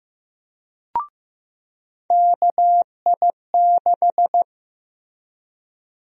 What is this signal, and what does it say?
Morse code, heard as a keyed tone, about 15 wpm, 700 Hz. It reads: KI6
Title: USA Call Sign Prefixes - 15wpm